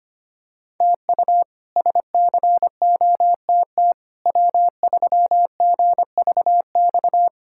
Morse code, heard as a keyed tone, about 25 wpm, 700 Hz. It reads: TU SCOTT W3GVX
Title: K1USN SST QSO Practice - 25wpm